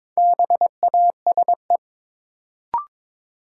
Morse code, heard as a keyed tone, about 22 wpm, 700 Hz. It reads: BASE
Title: Sentences from Top 1600 Words - New Word Drill Alt - 22wpm